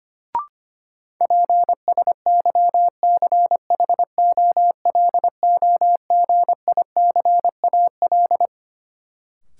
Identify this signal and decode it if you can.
Morse code, heard as a keyed tone, about 25 wpm, 700 Hz. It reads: PSYCHOLOGICAL
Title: Sentences from Top 1400 Words - New Word Drill - 25wpm